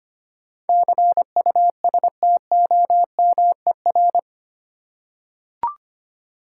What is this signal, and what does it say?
Morse code, heard as a keyed tone, about 25 wpm, 700 Hz. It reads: CUSTOMER
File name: Eight Letter Words - 25wpm